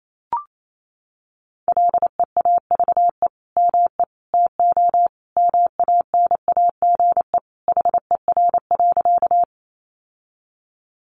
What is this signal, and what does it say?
Morse code, heard as a keyed tone, about 28 wpm, 700 Hz. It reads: LEAVE ME TO MANAGE HER.
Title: Sentences from Top 1400 Words 28wpm